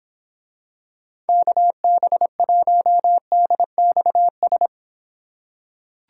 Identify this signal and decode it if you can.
Morse code, heard as a keyed tone, about 25 wpm, 700 Hz. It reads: KB1DXS